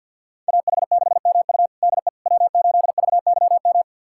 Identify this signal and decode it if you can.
Morse code, heard as a keyed tone, about 51 wpm, 700 Hz. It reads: AH6MU DE W8VYM
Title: USA Callsign DE Callsign - 5 Characters - 50wpm